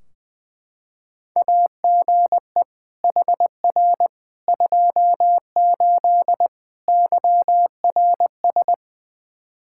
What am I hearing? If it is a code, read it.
Morse code, heard as a keyed tone, about 20 wpm, 700 Hz. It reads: AGE HR 28 YRS
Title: Practice QSO - Exchange 3 - More Info - 20wpm